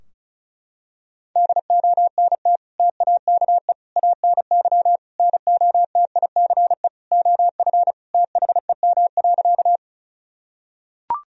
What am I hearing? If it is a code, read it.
Morse code, heard as a keyed tone, about 34 wpm, 700 Hz. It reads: DONT TAKE ANY NOTICE OF THEM.
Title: Sentences from Top 1000 Words - Review Part 2 - 35wpm